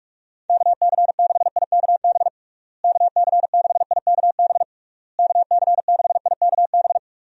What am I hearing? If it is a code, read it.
Morse code, heard as a keyed tone, about 46 wpm, 700 Hz. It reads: KC6IKB KC6IKB KC6IKB
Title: USA Call Signs - 6 Characters - Repeated 3x - 45wpm